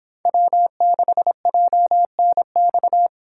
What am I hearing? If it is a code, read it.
Morse code, heard as a keyed tone, about 24 wpm, 700 Hz. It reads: W6JNX